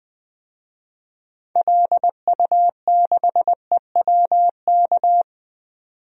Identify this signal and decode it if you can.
Morse code, heard as a keyed tone, about 20 wpm, 700 Hz. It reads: LU6EWK
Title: DXCC-100 Call Signs - 6 Characters - 20wpm